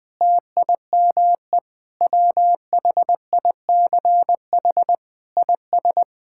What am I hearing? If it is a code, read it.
Morse code, heard as a keyed tone, about 20 wpm, 700 Hz. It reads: TIME WHICH IS
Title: Sentences with 8-Word Phrases - 20wpm